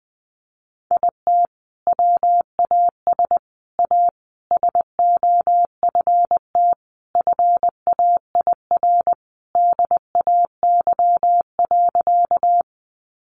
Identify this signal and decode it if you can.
Morse code, heard as a keyed tone, about 20 wpm, 700 Hz. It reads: IT WAS A SOFT FAIR DAY.
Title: Sentences from Top 1500 Words 20wpm